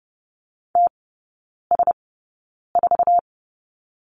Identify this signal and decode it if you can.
Morse code, heard as a keyed tone, about 28 wpm, spaced as for 9 wpm, 690 Hz. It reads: TS4
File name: Lesson 16e - Letter c - Three Characters 30wpm